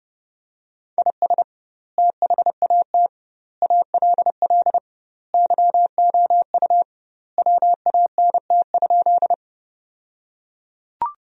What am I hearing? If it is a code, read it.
Morse code, heard as a keyed tone, about 30 wpm, 700 Hz. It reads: IS THAT ALL YOU WANT?
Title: 2x Word-Spacing - Sentences from Top 200 Words - 30wpm